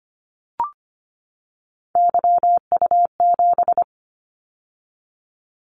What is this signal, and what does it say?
Morse code, heard as a keyed tone, about 25 wpm, 700 Hz. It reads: YU7
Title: Callsign Prefixes for Top 25 DXCC - 25wpm